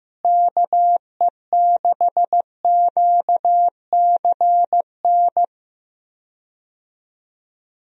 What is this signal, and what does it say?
Morse code, heard as a keyed tone, about 15 wpm, 700 Hz. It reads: KE6QCN
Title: USA Call Signs - 6 Characters - Repeated 2x - 15wpm